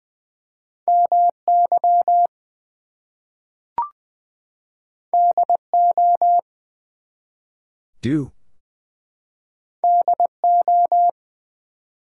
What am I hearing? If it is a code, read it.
Morse code, heard as a keyed tone, about 20 wpm, 700 Hz. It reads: MY DO DO
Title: Two Letter Words - 20wpm